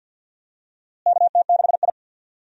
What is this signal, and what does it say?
Morse code, heard as a keyed tone, about 49 wpm, 700 Hz. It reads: KT6I
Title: USA Call Signs - 4 Characters - Repeated 3x - 50wpm